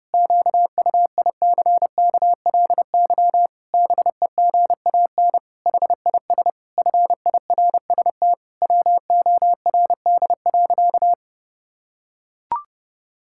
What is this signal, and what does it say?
Morse code, heard as a keyed tone, about 32 wpm, 700 Hz. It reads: QUICKLY BEGAN HIS FIRST WORD.
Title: Pangram Challenge v2 - 30wpm